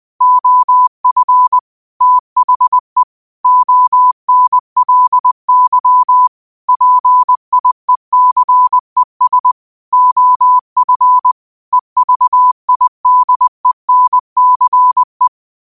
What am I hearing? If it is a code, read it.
Morse code, heard as a keyed tone, about 20 wpm, 1000 Hz. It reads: OF THE ONLY PIECES OF EVIDENCE
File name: qotdfiles / 20wpm